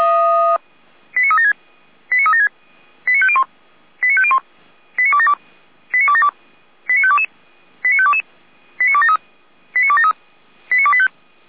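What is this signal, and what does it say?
Five-tone selective call (ZVEI), heard as quick runs of five tones, tones of about 70 ms. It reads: 89376 89376 89501 89501 89272 89272 89522 89522 89273 89273 89285
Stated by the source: ZVEI five-tone selective call received on an SDR, from the feuerwehr-orlamuende server (fire brigade Orlamünde).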